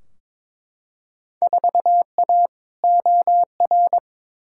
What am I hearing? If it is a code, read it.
Morse code, heard as a keyed tone, about 22 wpm, 700 Hz. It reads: 4A OR
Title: Field Day Exchanges Encore - 22wpm